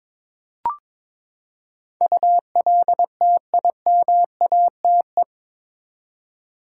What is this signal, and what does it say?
Morse code, heard as a keyed tone, about 22 wpm, 700 Hz. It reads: ULTIMATE
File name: practice / 22wpm / Eight Letter Words - 22wpm